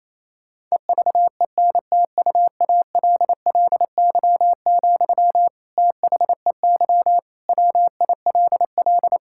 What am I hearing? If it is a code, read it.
Morse code, heard as a keyed tone, about 28 wpm, 700 Hz. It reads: EVENTUALLY, THEY WILL
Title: Sentences from Top 1000 Words - Review Part 6 - 28wpm